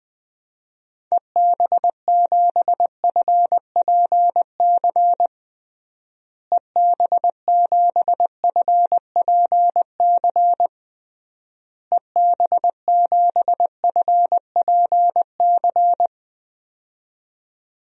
Morse code, heard as a keyed tone, about 21 wpm, 700 Hz. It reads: EB7FPC EB7FPC EB7FPC
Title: DXCC-100 Call Signs - 6 Characters - Repeated 3x - 20wpm